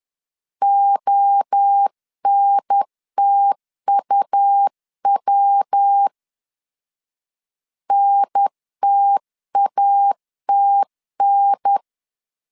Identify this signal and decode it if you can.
Morse code, heard as a keyed tone, about 10 wpm, 800 Hz. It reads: ONTUW NTATN
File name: 10wpm